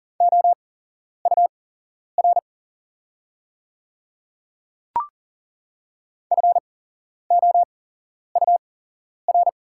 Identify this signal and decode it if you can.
Morse code, heard as a keyed tone, about 40 wpm, spaced as for 11 wpm, 700 Hz. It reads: OUR FOUR
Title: 8x Character Spacing - Top 300 Words - 40wpm